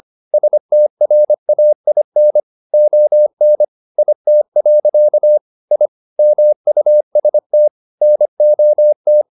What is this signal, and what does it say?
Morse code, heard as a keyed tone, about 25 wpm, 600 Hz. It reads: STRAIN ON IT. I MUST NOT
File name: the old man and the sea part 1 25wpm